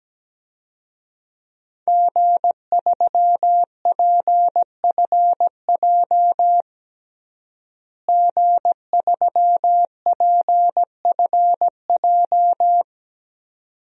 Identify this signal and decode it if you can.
Morse code, heard as a keyed tone, about 17 wpm, 700 Hz. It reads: G3PFJ G3PFJ
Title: DXCC-100 Call Signs - 5 Characters - Repeated 2x - 17wpm